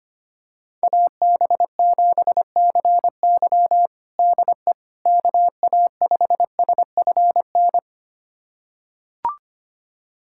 Morse code, heard as a keyed tone, about 25 wpm, 700 Hz. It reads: AB7CY DE KA5SFN
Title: USA Callsign DE Callsign 25wpm